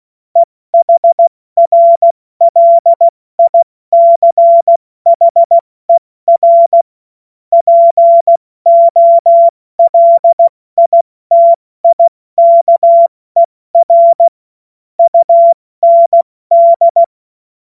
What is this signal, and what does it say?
Morse code, heard as a keyed tone, about 16 wpm, 680 Hz. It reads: EHRLICHER POLITIKER UND
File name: Morsekurs / Witze / Witz 03. Januar 2023 70 F80